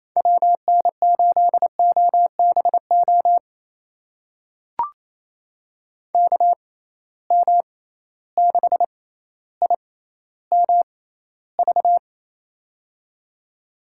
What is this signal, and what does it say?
Morse code, heard as a keyed tone, about 28 wpm, spaced as for 14 wpm, 700 Hz. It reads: WN8OBO KM6IMV
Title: USA Call Signs - 6 Characters - 6x Character Spacing - 28wpm